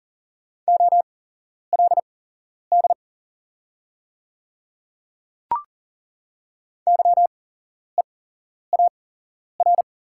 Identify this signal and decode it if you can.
Morse code, heard as a keyed tone, about 40 wpm, spaced as for 11 wpm, 690 Hz. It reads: OLD YEAR
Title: 8x Character Spacing - Top 300 Words - 40wpm